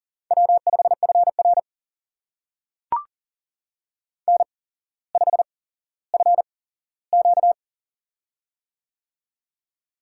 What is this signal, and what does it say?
Morse code, heard as a keyed tone, about 38 wpm, spaced as for 17 wpm, 700 Hz. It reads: W5FR N5FQ
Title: USA Call Signs - 4 Characters - 8x Character Spacing - 40wpm